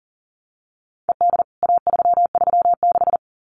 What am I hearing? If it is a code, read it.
Morse code, heard as a keyed tone, about 40 wpm, 700 Hz. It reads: ED A336
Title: CWT Exchange 40wpm